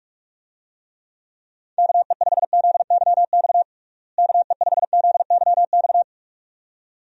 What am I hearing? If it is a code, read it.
Morse code, heard as a keyed tone, about 47 wpm, 700 Hz. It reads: KE5ZYX KE5ZYX
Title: USA Call Signs - 6 Characters - Repeated 2x - 45wpm